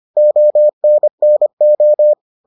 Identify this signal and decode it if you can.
Morse code, heard as a keyed tone, about 25 wpm, 600 Hz. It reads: ONNO